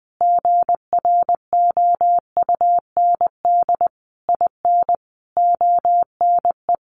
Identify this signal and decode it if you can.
Morse code, heard as a keyed tone, about 20 wpm, 700 Hz. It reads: GROUND IN ONE
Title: Sentences from Top 500 Words 20wpm